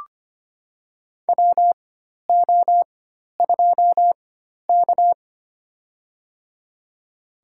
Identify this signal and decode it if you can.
Morse code, heard as a keyed tone, about 25 wpm, spaced as for 12 wpm, 700 Hz. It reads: WO2K
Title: USA Call Signs - 4x Character Spacing - 25wpm